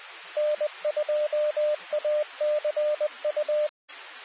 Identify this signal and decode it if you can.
Morse code, heard as a keyed tone, about 20 wpm, 600 Hz, with radio noise. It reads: N2ACU
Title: Nassau and Suffolk County Amateur Repeaters